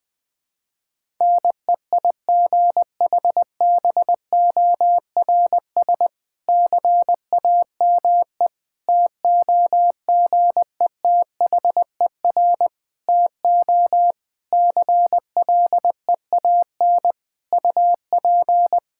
Morse code, heard as a keyed tone, about 20 wpm, 700 Hz. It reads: NEIGHBORS CAME TOGETHER TO CLEAN UP
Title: Difficult Sentences with Hints - 20wpm